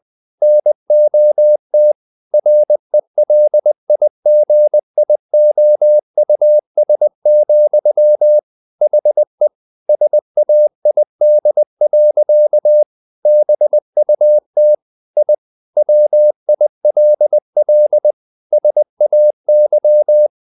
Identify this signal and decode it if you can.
Morse code, heard as a keyed tone, about 20 wpm, 600 Hz. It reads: NOT RELIGIOUS, HE SAID. BUT I WILL SAY
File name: the old man and the sea / the old man and the sea part 1 20wpm